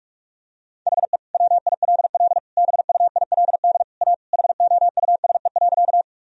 Morse code, heard as a keyed tone, about 46 wpm, 700 Hz. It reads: HE WILL BUILD A HOUSE.
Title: Sentences from Top 1000 Words - Review Part 1 - 45wpm